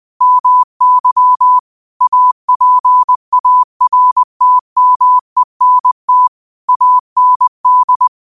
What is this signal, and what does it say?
Morse code, heard as a keyed tone, about 20 wpm, 1000 Hz. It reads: MY APARTMENT AND
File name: qotdfiles / 20wpm